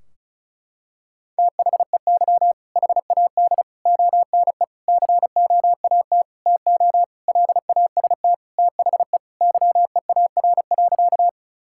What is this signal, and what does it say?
Morse code, heard as a keyed tone, about 34 wpm, 700 Hz. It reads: THEY HAD ONE COAT TO LAST THE YEAR.
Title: Sentences from Top 1600 Words 35wpm